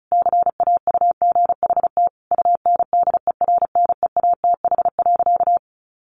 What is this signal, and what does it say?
Morse code, heard as a keyed tone, about 34 wpm, 700 Hz. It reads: CAUGHT UNDERNEATH.
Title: A Sound Of Thunder 35wpm